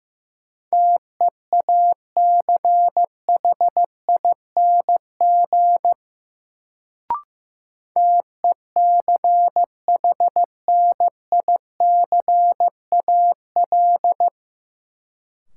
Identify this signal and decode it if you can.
Morse code, heard as a keyed tone, about 15 wpm, 700 Hz. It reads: TEACHING TECHNICAL
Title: Sentences from Top 1100 Words - New Word Drill Alt - 15wpm